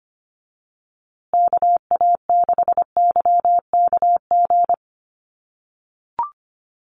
Morse code, heard as a keyed tone, about 25 wpm, 700 Hz. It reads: KA6YKG
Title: USA Call Signs Repeated 2x 25wpm